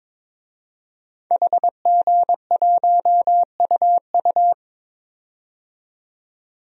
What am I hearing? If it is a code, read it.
Morse code, heard as a keyed tone, about 22 wpm, 700 Hz. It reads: HG1UU
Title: DXCC-100 Call Signs - 5 Characters - 22wpm